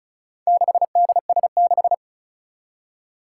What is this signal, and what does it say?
Morse code, heard as a keyed tone, about 34 wpm, 700 Hz. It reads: 6DS6
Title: Four Challenging Letter-Number Combinations 35wpm